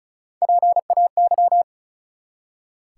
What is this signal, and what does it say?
Morse code, heard as a keyed tone, about 36 wpm, 700 Hz. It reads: PAY
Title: Three Letter Words - 35wpm